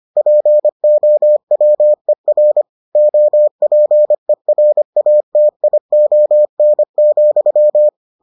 Morse code, heard as a keyed tone, about 25 wpm, 600 Hz. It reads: POWER OPERATION,